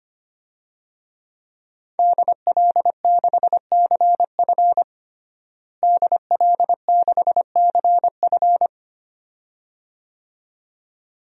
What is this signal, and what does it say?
Morse code, heard as a keyed tone, about 25 wpm, 700 Hz. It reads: DL6CF DL6CF
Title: DXCC-100 Call Signs - 5 Characters - Repeated 2x - 25wpm